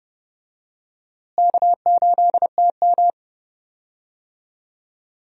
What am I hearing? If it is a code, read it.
Morse code, heard as a keyed tone, about 30 wpm, 700 Hz. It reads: K8TM
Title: USA Call Signs - 4 Characters - Repeated 2x - 30wpm